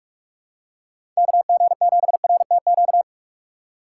Morse code, heard as a keyed tone, about 47 wpm, 700 Hz. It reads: KG7RTQ
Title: USA Call Signs - 6 Characters - 8x Character Spacing - 45wpm